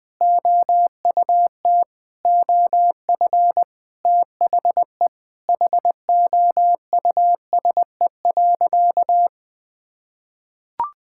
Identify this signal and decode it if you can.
Morse code, heard as a keyed tone, about 19 wpm, 700 Hz. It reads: OUT OF THE HOUSE.
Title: Sentences with 4-Word Phrases - Spoken Beginning - 20wpm